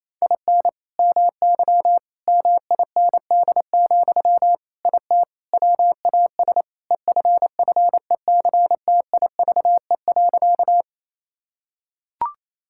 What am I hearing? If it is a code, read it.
Morse code, heard as a keyed tone, about 28 wpm, 700 Hz. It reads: IN MY MIND, IT WAS EFFECTIVE.
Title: Sentences from Top 1000 Words - Review Part 6 - 28wpm